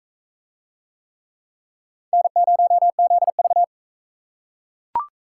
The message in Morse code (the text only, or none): N0ZV